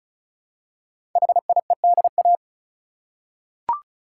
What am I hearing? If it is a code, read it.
Morse code, heard as a keyed tone, about 35 wpm, 700 Hz. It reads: HIEDA
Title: Code Groups - 5 Letters - 35wpm